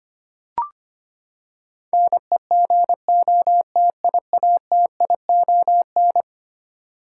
Morse code, heard as a keyed tone, about 25 wpm, 700 Hz. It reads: NEGOTIATION
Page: morse-code-ninja-practice-25wpm